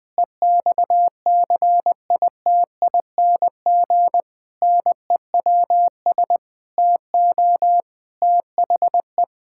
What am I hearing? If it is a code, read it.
Morse code, heard as a keyed tone, about 20 wpm, 700 Hz. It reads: EXCITING NEWS TO THE
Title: Pangram Challenge v2 - 20wpm